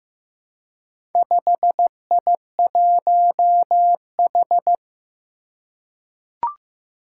Morse code, heard as a keyed tone, about 15 wpm, 700 Hz. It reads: 5I1H
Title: Four Challenging Letter-Number Combinations 15wpm